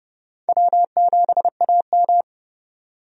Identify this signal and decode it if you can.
Morse code, heard as a keyed tone, about 30 wpm, 700 Hz. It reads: W7AM
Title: USA Call Signs - 4 Characters - 30wpm